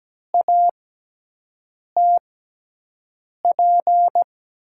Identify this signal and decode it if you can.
Morse code, heard as a keyed tone, about 17 wpm, spaced as for 6 wpm, 700 Hz. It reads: ATP
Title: USA Call Signs - 6 Characters - 6x Character Spacing - 17wpm